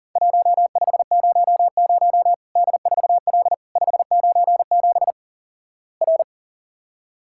Morse code, heard as a keyed tone, about 38 wpm, 700 Hz. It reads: 1500 D4L 597 R
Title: Worked All Europe DX Contest - QTC Exchange - 40wpm